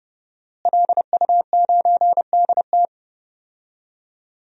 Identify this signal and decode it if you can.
Morse code, heard as a keyed tone, about 30 wpm, 700 Hz. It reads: LU9DT
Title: DXCC-100 Call Signs - 5 Characters - Repeated 2x - 30wpm